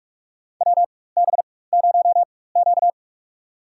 Morse code, heard as a keyed tone, about 43 wpm, spaced as for 21 wpm, 700 Hz. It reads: WB0Q
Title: USA Call Signs - 4 Characters - 4x Character Spacing - 45wpm